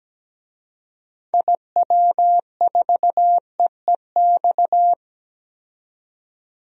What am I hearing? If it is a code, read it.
Morse code, heard as a keyed tone, about 17 wpm, 700 Hz. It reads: IW4EEX
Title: DXCC-100 Call Signs - 6 Characters - 17wpm